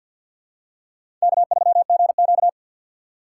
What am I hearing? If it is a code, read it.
Morse code, heard as a keyed tone, about 52 wpm, 700 Hz. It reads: K3GQ
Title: USA Call Signs - 4 Characters - 3x Character Spacing - 50wpm